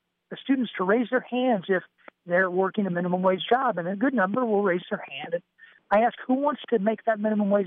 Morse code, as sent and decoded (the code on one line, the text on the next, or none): none